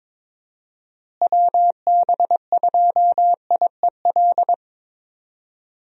Morse code, heard as a keyed tone, about 22 wpm, 700 Hz. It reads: WB2IEL
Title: USA Call Signs - 6 Characters - Repeated 2x - 22wpm